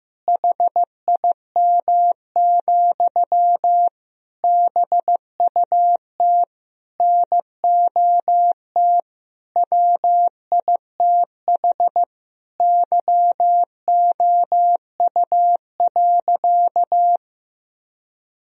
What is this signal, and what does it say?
Morse code, heard as a keyed tone, about 15 wpm, 700 Hz. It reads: HIM, BUT NOT WITH YOU.
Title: Sentences from Top 1000 Words - Review Part 4 - 15wpm